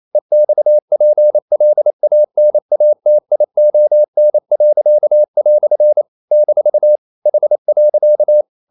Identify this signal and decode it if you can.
Morse code, heard as a keyed tone, about 28 wpm, 600 Hz. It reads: EXPLANATION." - H.
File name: CW / Practice / EnglishQuotations0228